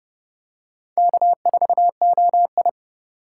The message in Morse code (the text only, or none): K4OI